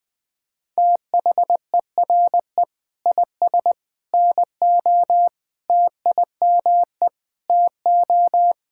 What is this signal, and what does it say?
Morse code, heard as a keyed tone, about 20 wpm, 700 Hz. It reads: THERE IS NO TIME TO
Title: Sentences with 3-Word Phrases - Spoken Beginning - 20wpm